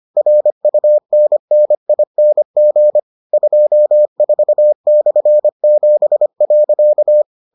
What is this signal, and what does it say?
Morse code, heard as a keyed tone, about 25 wpm, 600 Hz. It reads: RUNNING 24/7.